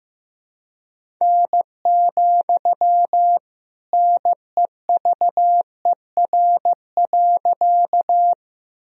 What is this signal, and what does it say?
Morse code, heard as a keyed tone, about 15 wpm, 700 Hz. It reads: N, NEVER.
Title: Sentences from Top 300 Words and Abbreviations 15wpm